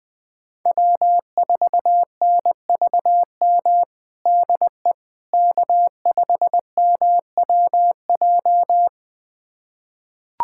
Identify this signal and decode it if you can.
Morse code, heard as a keyed tone, about 19 wpm, 700 Hz. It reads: W4NVM DE K5MWJ
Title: USA Callsign DE Callsign - 5 Characters - 20wpm